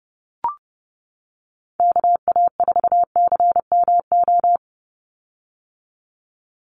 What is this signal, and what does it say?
Morse code, heard as a keyed tone, about 30 wpm, 700 Hz. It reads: KA4CMO
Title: USA Call Signs 30wpm